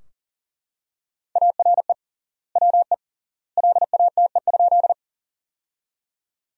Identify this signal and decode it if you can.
Morse code, heard as a keyed tone, about 40 wpm, 700 Hz. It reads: ARE WE LATE?
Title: 3x Word-Spacing - Sentences from Top 300 Words - 40wpm